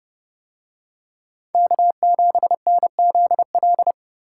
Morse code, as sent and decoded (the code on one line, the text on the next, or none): -.- --... -. --.. .-..
K7NZL